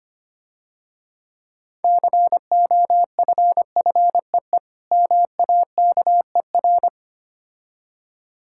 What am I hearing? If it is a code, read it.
Morse code, heard as a keyed tone, about 25 wpm, 700 Hz. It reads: COFFEE MAKER
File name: practice / 25wpm / Difficult Words - Speed-Racing with Hints - 25wpm